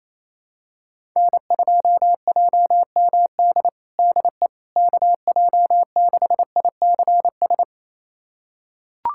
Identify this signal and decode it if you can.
Morse code, heard as a keyed tone, about 28 wpm, 700 Hz. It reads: N2JMD DE KJ6ICS
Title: USA Callsign DE Callsign 28wpm